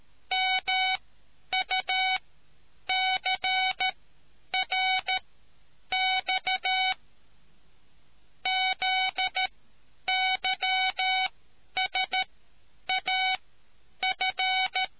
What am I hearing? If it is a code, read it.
Morse code, heard as a keyed tone, about 13 wpm, spaced as for 9 wpm, 750 Hz. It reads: MUCRX ZYSAF